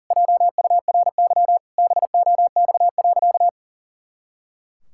Morse code, heard as a keyed tone, about 40 wpm, 700 Hz. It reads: JURY BOX.
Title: Pangram Challenge v2 - 40wpm